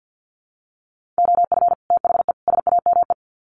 Morse code, heard as a keyed tone, about 48 wpm, 700 Hz. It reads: OF THE SAME
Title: Sets of 3 Words 50wpm